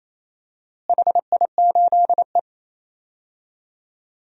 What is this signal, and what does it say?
Morse code, heard as a keyed tone, about 27 wpm, 700 Hz. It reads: HI8E
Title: DXCC-100 Call Signs - 4 Characters - Repeated 3x - 28wpm